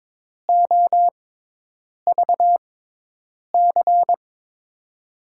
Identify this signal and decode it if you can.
Morse code, heard as a keyed tone, about 22 wpm, spaced as for 8 wpm, 700 Hz. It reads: OVC
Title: USA Call Signs - 6 Characters - 6x Character Spacing - 22wpm